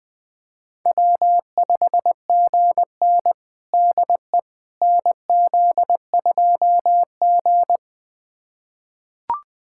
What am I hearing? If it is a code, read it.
Morse code, heard as a keyed tone, about 20 wpm, 700 Hz. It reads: W5GN DE NZ2G
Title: USA Callsign DE Callsign - 4 Characters - 20wpm